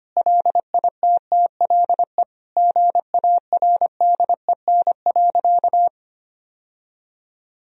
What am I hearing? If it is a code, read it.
Morse code, heard as a keyed tone, about 25 wpm, 700 Hz. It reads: LITTLE GARDEN.
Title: Sentences from Top 1000 Words - Review Part 4 - 25wpm